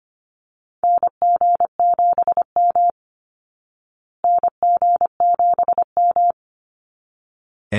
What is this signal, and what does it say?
Morse code, heard as a keyed tone, about 25 wpm, 700 Hz. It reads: NG7M NG7M
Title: Call Signs Repeated 3x - 25wpm